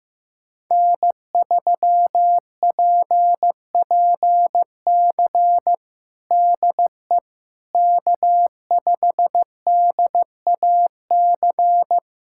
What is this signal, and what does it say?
Morse code, heard as a keyed tone, about 15 wpm, 700 Hz. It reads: N3PPC DE K5DAC
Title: K1USN SST QSO Practice 15wpm